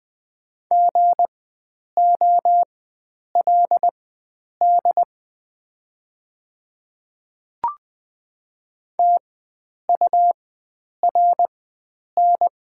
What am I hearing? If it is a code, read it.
Morse code, heard as a keyed tone, about 20 wpm, spaced as for 9 wpm, 690 Hz. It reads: GOLD TURN
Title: 4x Character Spacing - Top 500 Words - 20wpm